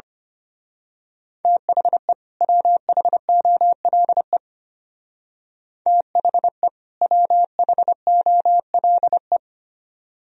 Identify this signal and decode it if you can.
Morse code, heard as a keyed tone, about 30 wpm, 700 Hz. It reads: THE WHOLE THE WHOLE
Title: Speed-Racing - Sets of 2 Words 25wpm